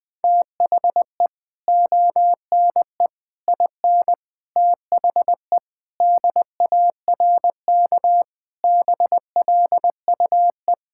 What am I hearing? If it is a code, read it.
Morse code, heard as a keyed tone, about 20 wpm, 700 Hz. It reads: THE ONE IN THE DARK BLUE
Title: Sentences from Top 1600 Words 20wpm